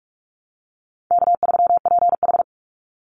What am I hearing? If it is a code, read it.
Morse code, heard as a keyed tone, about 44 wpm, 700 Hz. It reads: K3PH
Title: Easy Call Signs Repeated 2x - 45wpm